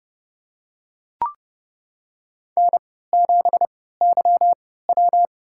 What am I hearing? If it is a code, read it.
Morse code, heard as a keyed tone, about 32 wpm, spaced as for 17 wpm, 700 Hz. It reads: N7YW
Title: USA Call Signs - 3x Character Spacing - 30wpm